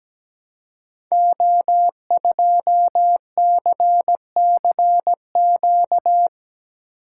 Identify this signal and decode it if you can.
Morse code, heard as a keyed tone, about 17 wpm, 700 Hz. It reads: O2CCQ
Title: Code Groups - 5 Letter:Numbers - 17wpm